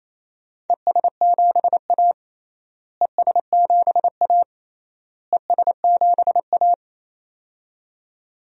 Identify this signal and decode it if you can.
Morse code, heard as a keyed tone, about 28 wpm, 700 Hz. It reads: ES7A ES7A ES7A
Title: DXCC-100 Call Signs - 4 Characters - Repeated 3x - 28wpm